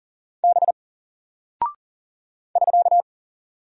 Morse code, heard as a keyed tone, about 40 wpm, 700 Hz. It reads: B <SK>
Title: Single Character (Letter:Number:Punctuation:Prosign) 40wpm